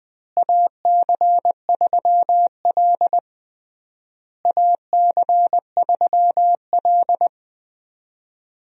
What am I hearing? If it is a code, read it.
Morse code, heard as a keyed tone, about 20 wpm, 700 Hz. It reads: AC3L AC3L
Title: USA Call Signs - 4 Characters - Repeated 3x - 20wpm